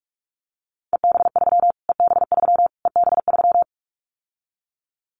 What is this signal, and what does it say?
Morse code, heard as a keyed tone, about 44 wpm, 700 Hz. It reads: EB3 EB3 EB3
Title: Callsign Prefixes for Top 25 DXCC - Repeated 3x - 45wpm